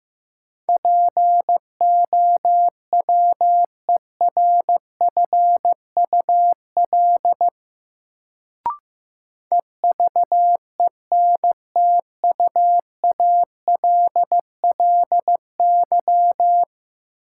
Morse code, heard as a keyed tone, about 15 wpm, 700 Hz. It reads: POWERFUL EVENTUALLY
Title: Sentences from Top 1000 Words - Review Part 6 - New Word Drill - 15wpm